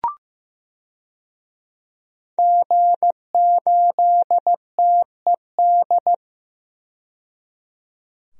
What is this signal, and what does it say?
Morse code, heard as a keyed tone, about 15 wpm, 700 Hz. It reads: G8TED
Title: DXCC-100 Call Signs - 5 Characters - 15wpm